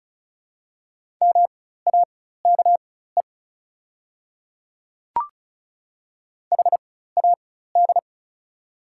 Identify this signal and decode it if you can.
Morse code, heard as a keyed tone, about 34 wpm, spaced as for 16 wpm, 690 Hz. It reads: MAKE HAD
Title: 4x Character Spacing - Top 500 Words - 35wpm